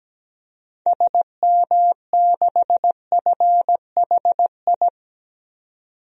Morse code, heard as a keyed tone, about 17 wpm, 700 Hz. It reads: SM6FHI
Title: DXCC-100 Call Signs - 6 Characters - 17wpm